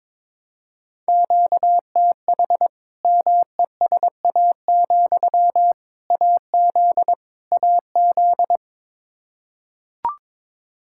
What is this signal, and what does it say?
Morse code, heard as a keyed tone, about 22 wpm, 700 Hz. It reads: QTH MESA, AZ AZ
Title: Practice QSO - Exchange 1 - Beginning - 22wpm